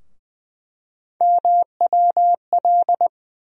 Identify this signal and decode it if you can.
Morse code, heard as a keyed tone, about 20 wpm, 700 Hz. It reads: MWL
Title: USA Call Sign Postfixes - 3 Letters 20wpm